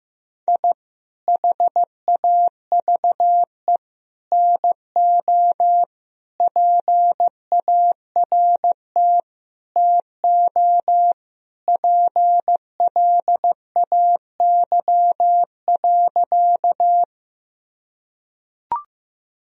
Morse code, heard as a keyed tone, about 15 wpm, 700 Hz. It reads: I HAVE NO PART TO PLAY.
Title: Sentences from Top 1000 Words - Review Part 1 - 15wpm